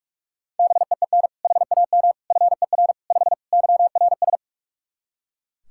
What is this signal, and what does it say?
Morse code, heard as a keyed tone, about 41 wpm, 700 Hz. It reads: BEEN HAM FER 5 YRS
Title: K1USN SST QSO Practice 45wpm